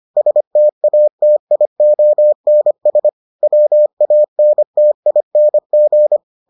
Morse code, heard as a keyed tone, about 25 wpm, 600 Hz. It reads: STATIONS WANTING